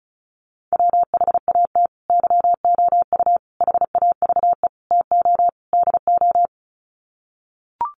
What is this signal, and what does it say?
Morse code, heard as a keyed tone, about 34 wpm, 700 Hz. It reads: WHAT YOU HAVE TO DO
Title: Sets of 5 Words Encore 35wpm